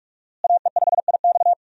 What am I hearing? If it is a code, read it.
Morse code, heard as a keyed tone, about 41 wpm, 700 Hz. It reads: AE5IX
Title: USA Call Signs - 5 Characters - Repeated 3x - 45wpm